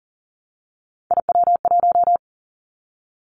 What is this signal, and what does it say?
Morse code, heard as a keyed tone, about 40 wpm, 700 Hz. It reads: IW1
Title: Callsign Prefixes for Top 25 DXCC - 40wpm